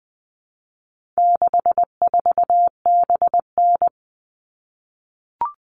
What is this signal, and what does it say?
Morse code, heard as a keyed tone, about 20 wpm, 700 Hz. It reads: 64BN
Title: All Cut Numbers - 4-Digit 20wpm